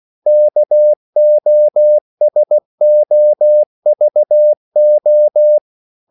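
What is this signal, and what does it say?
Morse code, heard as a keyed tone, about 16 wpm, 600 Hz. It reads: KOSOVO
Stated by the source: CW practice of national capitals at 16 words per minute, from ON6ZQ.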